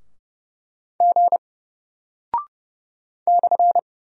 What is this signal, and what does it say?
Morse code, heard as a keyed tone, about 30 wpm, 700 Hz. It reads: G /
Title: Single Character (Letter:Number:Punctuation:Prosign) Alt 30wpm